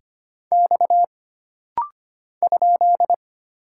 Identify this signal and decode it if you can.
Morse code, heard as a keyed tone, about 25 wpm, 700 Hz. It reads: X ?
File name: Single Character (Letter:Number:Punctuation:Prosign) - Rapid-Fire - 25wpm